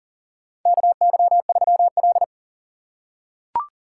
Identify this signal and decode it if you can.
Morse code, heard as a keyed tone, about 38 wpm, 690 Hz. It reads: KY3L
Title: USA Call Signs - 4 Characters - 2x Character Spacing - 40wpm